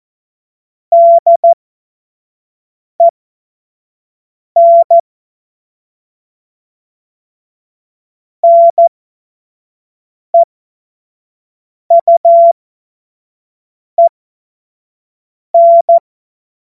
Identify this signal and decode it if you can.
Morse code, heard as a keyed tone, about 14 wpm, spaced as for 5 wpm, 680 Hz. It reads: DEN NEUEN
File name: Morsekurs / Witze / Witz 18. April 2023 25 F70